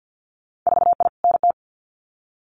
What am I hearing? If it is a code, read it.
Morse code, heard as a keyed tone, about 50 wpm, 700 Hz. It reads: 4I NT
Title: Winter Field Day Exchanges Encore 50wpm